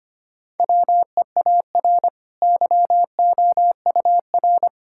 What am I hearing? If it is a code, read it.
Morse code, heard as a keyed tone, about 25 wpm, 700 Hz. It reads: WEAR YOUR
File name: Easy Sentences with Hints - 25wpm